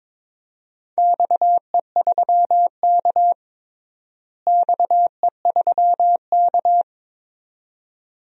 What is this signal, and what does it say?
Morse code, heard as a keyed tone, about 22 wpm, 700 Hz. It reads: XE3K XE3K
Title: DXCC-100 Call Signs - 4 Characters - Repeated 2x - 22wpm